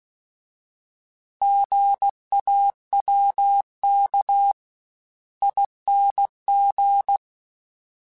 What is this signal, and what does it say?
Morse code, heard as a keyed tone, about 16 wpm, 800 Hz. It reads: GAWK ING